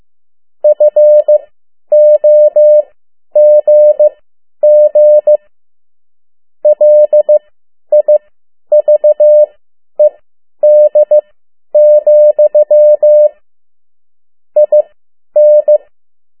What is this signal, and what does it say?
Morse code, heard as a keyed tone, about 15 wpm, spaced as for 10 wpm, 600 Hz. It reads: FOGG LIVED, IN
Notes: Around the World in 80 Days paragraph one Morse code 10 WPM